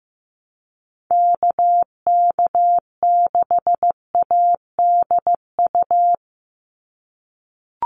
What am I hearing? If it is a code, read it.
Morse code, heard as a keyed tone, about 15 wpm, 700 Hz. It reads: KK6ADU
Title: USA Call Signs 15wpm